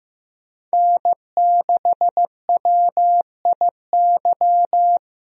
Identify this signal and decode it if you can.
Morse code, heard as a keyed tone, about 15 wpm, 700 Hz. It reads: N6WIY
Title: USA Call Signs - 5 Characters - 15wpm